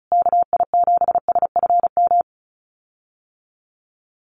Morse code, heard as a keyed tone, about 34 wpm, 700 Hz. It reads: KI7SFM
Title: USA Call Signs Repeated 3x 35wpm